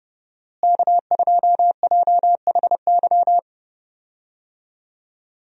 Morse code, heard as a keyed tone, about 30 wpm, 700 Hz. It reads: K2JHY
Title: USA Call Signs - 5 Characters - Repeated 2x - 30wpm